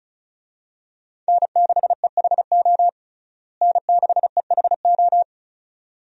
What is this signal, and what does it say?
Morse code, heard as a keyed tone, about 36 wpm, 700 Hz. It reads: N6EHO N6EHO
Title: USA Call Signs - 5 Characters - Repeated 2x - 35wpm